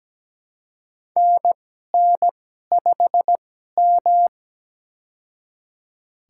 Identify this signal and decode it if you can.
Morse code, heard as a keyed tone, about 17 wpm, spaced as for 12 wpm, 700 Hz. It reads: NN5M
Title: USA Call Signs - 4 Characters - 2x Character Spacing - 17wpm